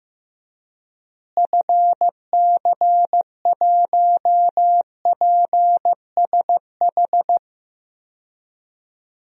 Morse code, heard as a keyed tone, about 15 wpm, 700 Hz. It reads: FC1PSH
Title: DXCC-100 Call Signs - 6 Characters - Repeated 2x - 15wpm